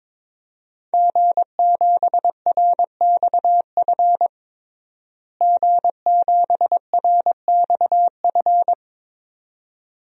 Morse code, heard as a keyed tone, about 22 wpm, 700 Hz. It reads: G7RXF G7RXF
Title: DXCC-100 Call Signs - 5 Characters - Repeated 2x - 22wpm